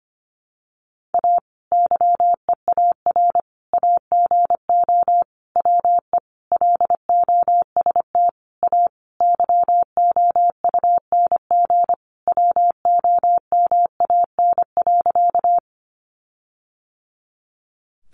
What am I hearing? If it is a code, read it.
Morse code, heard as a keyed tone, about 25 wpm, 700 Hz. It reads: A YEAR AGO WE LOST A YOUNG WOMAN.
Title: Sentences from Top 1400 Words 25wpm